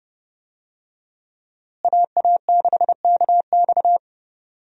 Morse code, heard as a keyed tone, about 30 wpm, 700 Hz. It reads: AA6KX
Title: USA Call Signs - 5 Characters - Repeated 2x - 30wpm